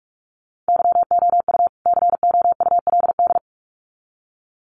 Morse code, heard as a keyed tone, about 44 wpm, 700 Hz. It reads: YOU COULD
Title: Sets of 2 Words 45wpm